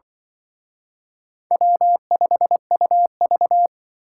Morse code, heard as a keyed tone, about 25 wpm, 700 Hz. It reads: W5UV
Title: Speed Racing - USA 1x2 Call Signs 20wpm